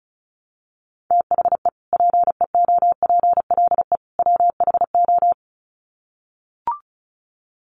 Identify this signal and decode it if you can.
Morse code, heard as a keyed tone, about 34 wpm, 700 Hz. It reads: THE PEOPLE WHO
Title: Sets of 3 Words 35wpm